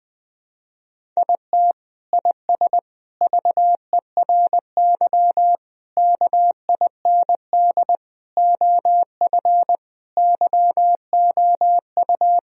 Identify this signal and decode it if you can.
Morse code, heard as a keyed tone, about 20 wpm, 700 Hz. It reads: IT IS VERY KIND OF YOU
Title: Sentences from Top 1000 Words - Review Part 1 - 20wpm